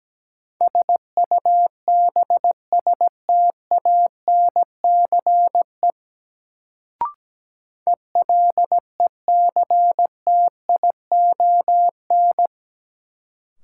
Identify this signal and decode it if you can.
Morse code, heard as a keyed tone, about 17 wpm, 700 Hz. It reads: SUBSTANCE ELECTION
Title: Sentences from Top 1500 Words - New Word Drill - 17wpm